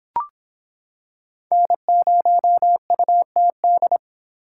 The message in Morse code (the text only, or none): N0UTD